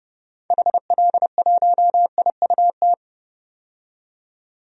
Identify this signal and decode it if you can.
Morse code, heard as a keyed tone, about 30 wpm, 700 Hz. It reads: HL1IUT
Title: DXCC-100 Call Signs - 6 Characters - 30wpm